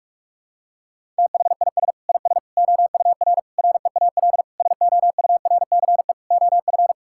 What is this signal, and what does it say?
Morse code, heard as a keyed tone, about 46 wpm, 700 Hz. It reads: THIS IS OUR REAL SOURCE OF
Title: Sentences from Top 1000 Words - Review Part 5 - 45wpm